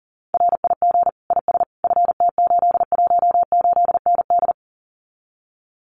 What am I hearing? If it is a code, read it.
Morse code, heard as a keyed tone, about 40 wpm, 700 Hz. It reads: RIG IS FT818ND
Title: Common QSO Phrases Encore 40wpm